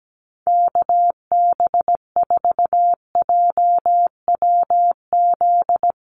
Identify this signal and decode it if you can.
Morse code, heard as a keyed tone, about 17 wpm, 700 Hz. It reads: KB4JWZ
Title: USA Call Signs Repeated 2x 17wpm